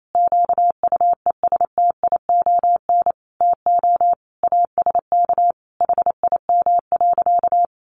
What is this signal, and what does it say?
Morse code, heard as a keyed tone, about 28 wpm, 700 Hz. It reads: QUESTION TO ASK HIM.
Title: Sentences from Top 1000 Words 28wpm